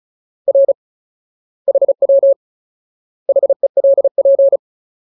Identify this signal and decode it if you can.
Morse code, heard as a keyed tone, about 34 wpm, 550 Hz. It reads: R HW HELP
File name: Ultimate Word List / 35WPM Ultimate